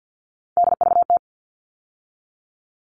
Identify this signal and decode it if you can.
Morse code, heard as a keyed tone, about 50 wpm, 700 Hz. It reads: DVT
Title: All Cut Numbers - 3-Digit 50wpm